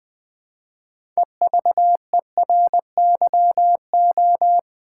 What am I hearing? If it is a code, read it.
Morse code, heard as a keyed tone, about 20 wpm, 700 Hz. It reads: EVERYO
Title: Build A Word - 3 Letters At A Time - 8-Letter Words - 20wpm